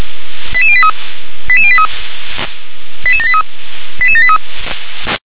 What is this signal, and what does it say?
Five-tone selective call (ZVEI), heard as quick runs of five tones, tones of about 70 ms. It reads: 88083 88083 88783 88783
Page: SDR/ZVEI